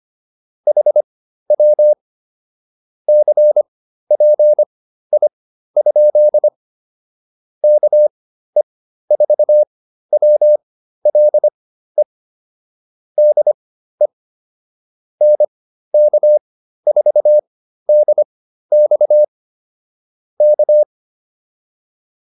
Morse code, heard as a keyed tone, about 25 wpm, spaced as for 13 wpm, 600 Hz. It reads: HW CPI? KE4WLE DE NK4DX K